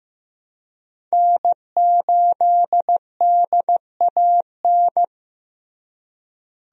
Morse code, heard as a keyed tone, about 15 wpm, 700 Hz. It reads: N8DAN
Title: USA Call Signs - 5 Characters - Repeated 2x - 15wpm